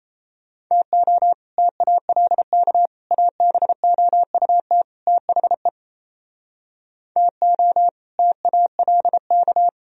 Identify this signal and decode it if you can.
Morse code, heard as a keyed tone, about 36 wpm, 700 Hz. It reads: TO TALK ABOUT THE TO TALK